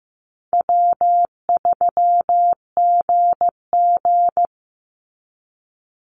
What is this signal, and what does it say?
Morse code, heard as a keyed tone, about 15 wpm, 700 Hz. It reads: W3GG
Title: Easy Call Signs - 15wpm